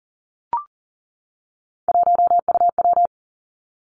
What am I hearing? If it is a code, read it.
Morse code, heard as a keyed tone, about 42 wpm, 700 Hz. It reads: 1UW
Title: Three Challenging Letter-Number Combinations 40wpm